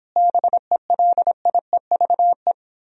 Morse code, heard as a keyed tone, about 31 wpm, 700 Hz. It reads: BELIEVE